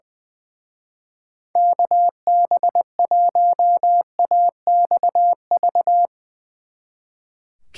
Morse code, heard as a keyed tone, about 21 wpm, 700 Hz. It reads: KB1AXV
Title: Speed Racing - USA 2x3 Call Signs 20wpm